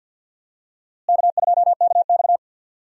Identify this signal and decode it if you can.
Morse code, heard as a keyed tone, about 48 wpm, 700 Hz. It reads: K2KX
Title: USA Call Signs - 4 Characters - Repeated 3x - 50wpm